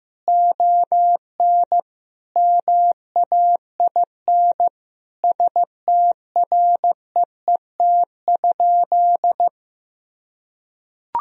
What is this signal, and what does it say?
Morse code, heard as a keyed tone, about 15 wpm, 700 Hz. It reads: ON MAIN STREET?
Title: Sentences from Top 1000 Words - Review Part 2 - 15wpm